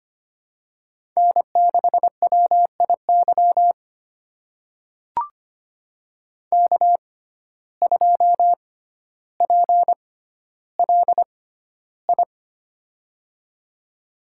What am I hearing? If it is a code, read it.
Morse code, heard as a keyed tone, about 25 wpm, spaced as for 13 wpm, 700 Hz. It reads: N6WIY K2PLI
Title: USA Call Signs - 5 Characters - 6x Character Spacing - 25wpm